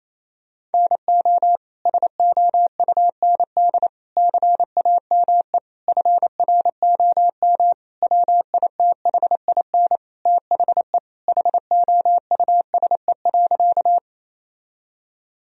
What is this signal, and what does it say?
Morse code, heard as a keyed tone, about 28 wpm, 700 Hz. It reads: NO SOUND CAME FROM WITHIN THE HOUSE.
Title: Sentences from Top 1000 Words - Review Part 4 - 28wpm